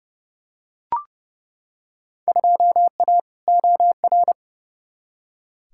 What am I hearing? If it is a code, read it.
Morse code, heard as a keyed tone, about 31 wpm, 700 Hz. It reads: 2A OR
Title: Field Day Exchanges - 30wpm